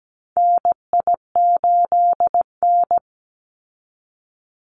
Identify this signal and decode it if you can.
Morse code, heard as a keyed tone, about 17 wpm, 700 Hz. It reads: NI8N
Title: Call Signs Repeated 2x - 17wpm